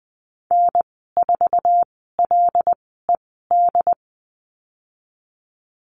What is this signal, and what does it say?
Morse code, heard as a keyed tone, about 20 wpm, spaced as for 14 wpm, 700 Hz. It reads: N4LED
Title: 076 Lesson 13g - Letter L - Call Signs 20wpm